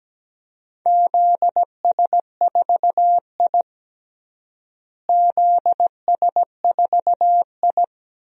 Morse code, heard as a keyed tone, about 17 wpm, 700 Hz. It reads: ZS4I ZS4I
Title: DXCC-100 Call Signs - 4 Characters - Repeated 3x - 17wpm